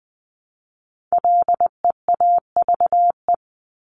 Morse code, heard as a keyed tone, about 20 wpm, 700 Hz. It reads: LEAVE
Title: Top 500 Words Repeated 20wpm